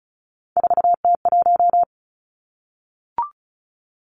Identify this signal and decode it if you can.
Morse code, heard as a keyed tone, about 36 wpm, 700 Hz. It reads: 4T1
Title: Three Challenging Letter-Number Combinations 35wpm